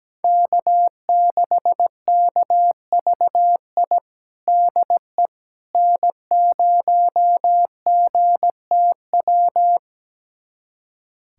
Morse code, heard as a keyed tone, about 17 wpm, 700 Hz. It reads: K6KVI DE N0GTW
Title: USA Callsign DE Callsign - 5 Characters - 17wpm